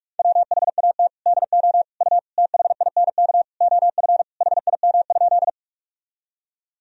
Morse code, heard as a keyed tone, about 46 wpm, 700 Hz. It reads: WHAT DO U THINK OF HIM?
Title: Sentences from Top 200 Words and Abbreviations 45wpm